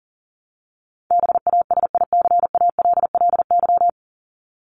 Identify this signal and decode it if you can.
Morse code, heard as a keyed tone, about 40 wpm, 700 Hz. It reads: BASICALLY
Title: Top 1251-1500 Words 40wpm